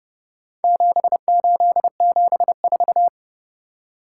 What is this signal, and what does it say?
Morse code, heard as a keyed tone, about 30 wpm, 700 Hz. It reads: 7874